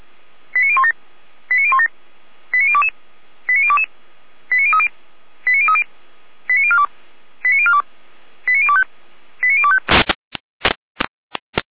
Five-tone selective call (ZVEI), heard as quick runs of five tones, tones of about 80 ms. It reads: 89017 89017 89022 89022 89030 89030 89042 89042 89025 89025
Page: SDR/ZVEI